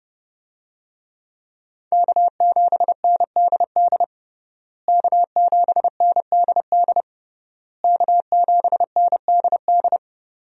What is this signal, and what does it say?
Morse code, heard as a keyed tone, about 32 wpm, 700 Hz. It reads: K7NDD K7NDD K7NDD
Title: USA Call Signs - 5 Characters - Repeated 3x - 30wpm